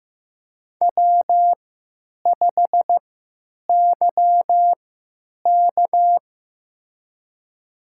Morse code, heard as a keyed tone, about 15 wpm, spaced as for 9 wpm, 700 Hz. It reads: W5YK
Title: USA Call Signs - 4 Characters - 3x Character Spacing - 15wpm